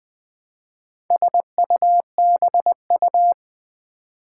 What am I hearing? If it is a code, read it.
Morse code, heard as a keyed tone, about 20 wpm, 700 Hz. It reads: SUBU
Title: Build A Word - 2 Letters At A Time - 6-Letter Words - 20wpm